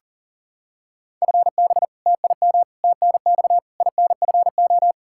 Morse code, heard as a keyed tone, about 40 wpm, 700 Hz. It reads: FB TIM TNX INFO
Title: K1USN SST QSO Practice 40wpm